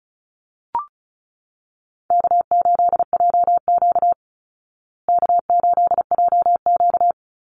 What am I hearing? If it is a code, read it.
Morse code, heard as a keyed tone, about 34 wpm, 700 Hz. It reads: K8JQ K8JQ
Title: Easy Call Signs Repeated 2x - 35wpm